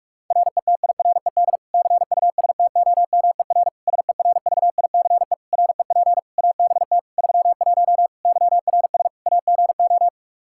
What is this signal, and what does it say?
Morse code, heard as a keyed tone, about 46 wpm, 700 Hz. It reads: RETIRED CUSTOMER SERVICE REP ABT 31 YRS AGO
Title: Practice QSO - Exchange 3 - More Info - 45wpm